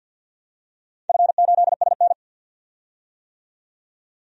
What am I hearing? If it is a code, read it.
Morse code, heard as a keyed tone, about 47 wpm, 700 Hz. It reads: F8SN
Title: DXCC-100 Call Signs - 4 Characters - 50wpm